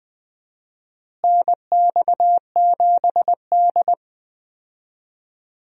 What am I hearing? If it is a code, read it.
Morse code, heard as a keyed tone, about 20 wpm, 700 Hz. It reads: NX7D
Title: USA Call Signs - 4 Characters - Repeated 3x - 20wpm